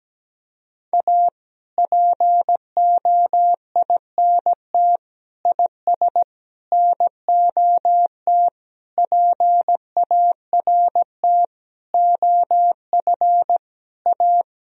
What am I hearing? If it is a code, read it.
Morse code, heard as a keyed tone, about 17 wpm, 700 Hz. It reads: A POINT IS NOT PART OF A
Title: Sentences from Top 1000 Words - Review Part 1 - 17wpm